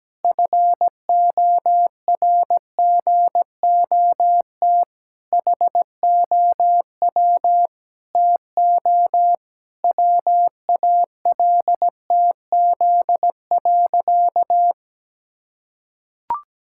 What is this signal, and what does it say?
Morse code, heard as a keyed tone, about 17 wpm, 700 Hz. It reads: FORGOT HOW TO WALTZ.
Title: Pangram Challenge v2 - 17wpm